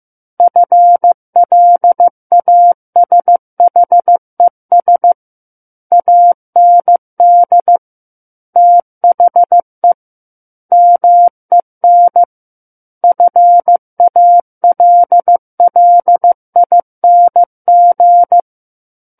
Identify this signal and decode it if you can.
Morse code, heard as a keyed tone, about 15 wpm, 700 Hz. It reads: FLASHES AND THE MEN FALLING
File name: Book One / 06-Chapter 6 - 15 WPM